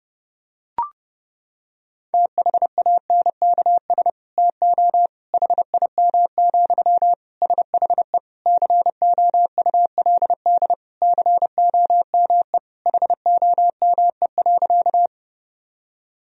Morse code, heard as a keyed tone, about 30 wpm, 700 Hz. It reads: THANKS TO HIM, SHE COULD COME HOME.
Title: Sentences from Top 1000 Words - Review Part 3 - 30wpm